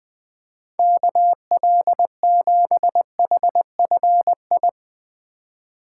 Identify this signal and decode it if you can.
Morse code, heard as a keyed tone, about 20 wpm, 700 Hz. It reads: KL7HFI
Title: K1USN SST QSO Practice - 20wpm